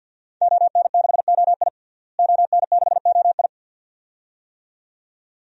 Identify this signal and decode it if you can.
Morse code, heard as a keyed tone, about 52 wpm, 700 Hz. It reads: ON6OI ON6OI
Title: DXCC-100 Call Signs - 5 Characters - Repeated 2x - 50wpm